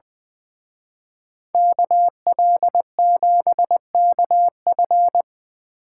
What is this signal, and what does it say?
Morse code, heard as a keyed tone, about 24 wpm, 700 Hz. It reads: KL7KF
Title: Speed Racing - USA 2x2 Call Signs 20wpm